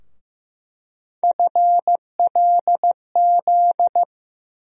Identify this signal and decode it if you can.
Morse code, heard as a keyed tone, about 16 wpm, 700 Hz. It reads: FLZ
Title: USA Call Sign Postfixes - 3 Letters 15wpm